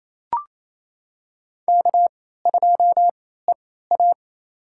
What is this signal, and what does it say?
Morse code, heard as a keyed tone, about 28 wpm, spaced as for 16 wpm, 700 Hz. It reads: K2EA
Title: USA Call Signs - 4 Characters - 3x Character Spacing - 28wpm